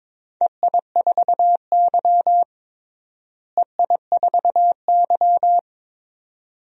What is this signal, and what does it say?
Morse code, heard as a keyed tone, about 22 wpm, 700 Hz. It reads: EI4Y EI4Y
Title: DXCC-100 Call Signs - 4 Characters - Repeated 2x - 22wpm